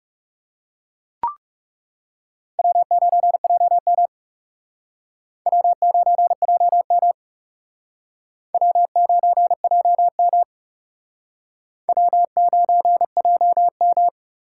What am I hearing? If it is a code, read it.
Morse code, heard as a keyed tone, about 38 wpm, 700 Hz. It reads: W9JM W9JM W9JM W9JM